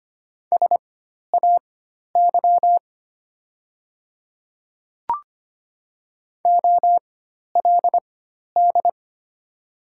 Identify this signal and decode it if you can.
Morse code, heard as a keyed tone, about 25 wpm, spaced as for 12 wpm, 690 Hz. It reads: SAY OLD
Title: 4x Character Spacing - Top 200 Words - 25wpm